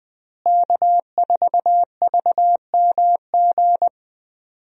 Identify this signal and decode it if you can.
Morse code, heard as a keyed tone, about 21 wpm, 700 Hz. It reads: K4VMG
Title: USA Call Signs - 5 Characters - Repeated 2x - 20wpm